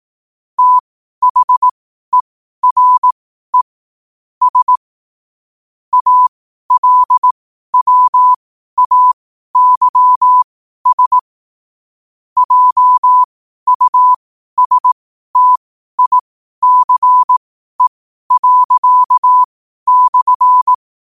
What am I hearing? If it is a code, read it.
Morse code, heard as a keyed tone, about 18 wpm, spaced as for 12 wpm, 1000 Hz. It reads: THERE S ALWAYS JUSTICE./
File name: qotdfiles / 12wpm